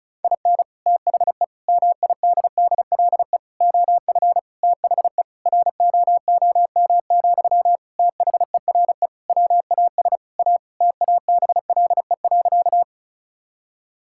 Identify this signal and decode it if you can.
Morse code, heard as a keyed tone, about 34 wpm, 700 Hz. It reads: IN THE MIDDLE OF THE ROOM, THERE WAS A TABLE.
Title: Sentences from Top 1000 Words - Review Part 6 - 35wpm